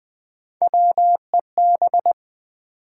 Morse code, heard as a keyed tone, about 20 wpm, 700 Hz. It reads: WEB
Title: Build A Word - 1 Letter At A Time - 3-Letter Words - 20wpm